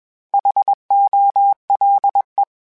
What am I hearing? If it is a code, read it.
Morse code, heard as a keyed tone, about 21 wpm, 800 Hz. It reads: HOLE